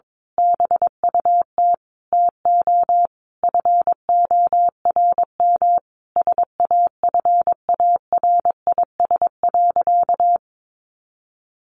Morse code, heard as a keyed tone, about 22 wpm, 700 Hz. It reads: BUT TO FORM SAFARIS.
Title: A Sound Of Thunder 22wpm